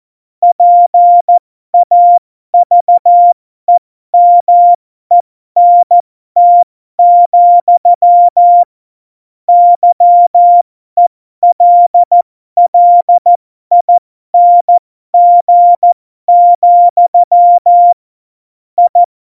Morse code, heard as a keyed tone, about 14 wpm, spaced as for 12 wpm, 700 Hz. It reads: PAVEMENT, YELLING, I
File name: Morsekurs / Witze